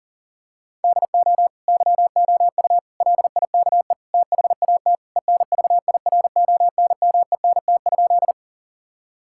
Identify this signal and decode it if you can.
Morse code, heard as a keyed tone, about 40 wpm, 700 Hz. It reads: DO YOU LIKE THAT ENVIRONMENT?
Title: Sentences from Top 1000 Words - Review Part 5 - 40wpm